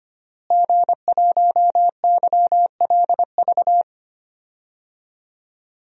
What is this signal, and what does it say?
Morse code, heard as a keyed tone, about 25 wpm, 700 Hz. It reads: G1YLV
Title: DXCC-100 Call Signs - 5 Characters - Repeated 3x - 25wpm